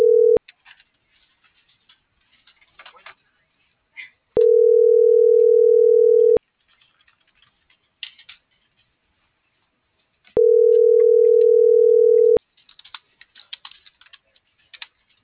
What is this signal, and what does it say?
Ringback tone (North America), 440 + 480 Hz.